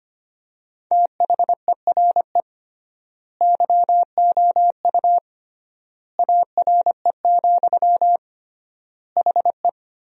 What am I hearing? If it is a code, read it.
Morse code, heard as a keyed tone, about 25 wpm, 700 Hz. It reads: THERE YOU ARE, HE
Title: 3x Word-Spacing - Sentences from Top 100 Words - 25wpm